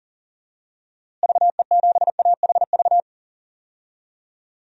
Morse code, heard as a keyed tone, about 37 wpm, 700 Hz. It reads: VE7AHV
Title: DXCC-100 Call Signs - 6 Characters - Repeated 2x - 40wpm